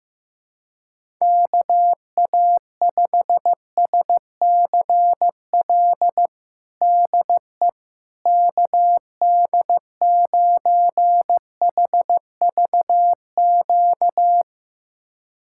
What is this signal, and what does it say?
Morse code, heard as a keyed tone, about 16 wpm, 700 Hz. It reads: KA5SCL DE KD9HVQ
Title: USA Callsign DE Callsign - 6 Characters - 15wpm